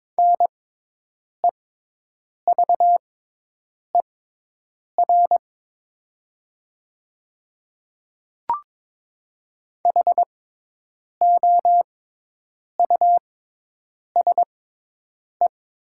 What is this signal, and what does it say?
Morse code, heard as a keyed tone, about 21 wpm, spaced as for 8 wpm, 700 Hz. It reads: NEVER HOUSE
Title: 6x Character Spacing - Top 300 Words - 22wpm